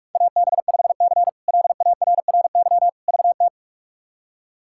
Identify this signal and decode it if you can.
Morse code, heard as a keyed tone, about 46 wpm, 700 Hz. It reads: AB5C LARRY VT
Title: K1USN SST QSO Practice - 45wpm